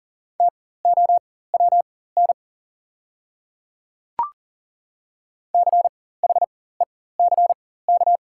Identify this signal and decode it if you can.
Morse code, heard as a keyed tone, about 40 wpm, spaced as for 19 wpm, 700 Hz. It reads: TOWN CHECK
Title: 4x Character Spacing - Top 500 Words - 40wpm